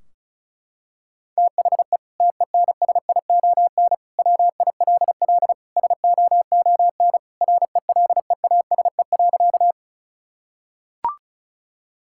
Morse code, heard as a keyed tone, about 36 wpm, 700 Hz. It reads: THE TENSION WILL SOON RELEASE.
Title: Sentences from Top 1600 Words 35wpm